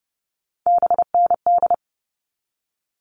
Morse code, heard as a keyed tone, about 30 wpm, 700 Hz. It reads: BND
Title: All Cut Numbers - 3-Digit 30wpm